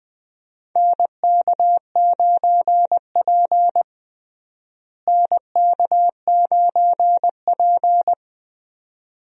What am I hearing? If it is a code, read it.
Morse code, heard as a keyed tone, about 20 wpm, 700 Hz. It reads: NK9P NK9P
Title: USA Call Signs - 4 Characters - Repeated 2x - 20wpm